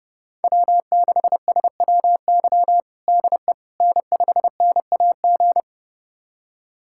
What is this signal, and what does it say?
Morse code, heard as a keyed tone, about 30 wpm, 700 Hz. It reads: W6SWY DE N5NAG
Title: USA Callsign DE Callsign 30wpm